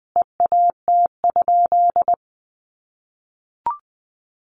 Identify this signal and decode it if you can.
Morse code, heard as a keyed tone, about 20 wpm, 700 Hz. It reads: EAT?
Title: Sentences from Top 500 Words 20wpm